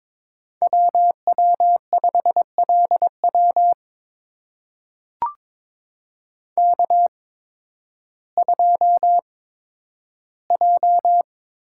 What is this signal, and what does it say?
Morse code, heard as a keyed tone, about 22 wpm, spaced as for 12 wpm, 700 Hz. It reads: WW5LW K2J
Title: USA Call Signs - 5 Characters - 8x Character Spacing - 22wpm